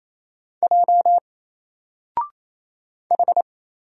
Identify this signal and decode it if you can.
Morse code, heard as a keyed tone, about 27 wpm, 700 Hz. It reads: J H
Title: Single Letter-Number Alt 28wpm